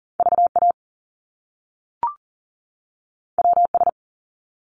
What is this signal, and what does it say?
Morse code, heard as a keyed tone, about 40 wpm, 700 Hz. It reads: VA WS
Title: Top Two Letter Combinations in Words 40wpm